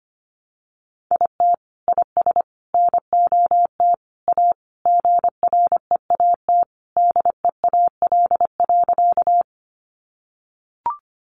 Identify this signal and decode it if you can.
Morse code, heard as a keyed tone, about 25 wpm, 700 Hz. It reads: IT IS NOT A GREAT DEAL.
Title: Sentences from Top 800 Words 25wpm